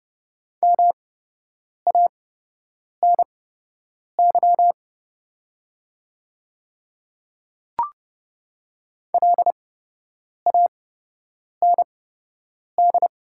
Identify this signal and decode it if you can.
Morse code, heard as a keyed tone, about 30 wpm, spaced as for 8 wpm, 690 Hz. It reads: MANY LAND
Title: 8x Character Spacing - Top 200 Words - 30wpm